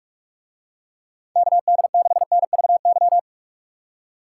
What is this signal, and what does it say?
Morse code, heard as a keyed tone, about 47 wpm, 700 Hz. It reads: KD6NVY